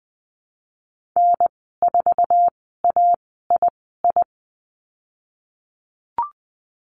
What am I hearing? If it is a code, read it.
Morse code, heard as a keyed tone, about 20 wpm, spaced as for 15 wpm, 700 Hz. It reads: N4AII
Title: 055 Lesson 10g - Letter r - Call Signs 20wpm